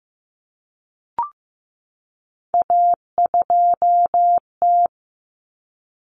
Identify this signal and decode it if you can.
Morse code, heard as a keyed tone, about 15 wpm, 700 Hz. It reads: A2T
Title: Common Cut Numbers - 3-Digit 15wpm